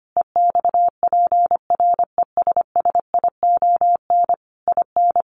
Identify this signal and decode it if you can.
Morse code, heard as a keyed tone, about 25 wpm, 700 Hz. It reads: EXPRESSION IN
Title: Sentences from Top 1200 Words 25wpm